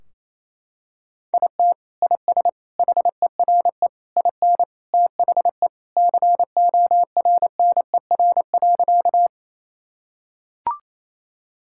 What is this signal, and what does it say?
Morse code, heard as a keyed tone, about 28 wpm, 700 Hz. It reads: IT IS HERE IN THE CORNER.
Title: Sentences from Top 1600 Words 28wpm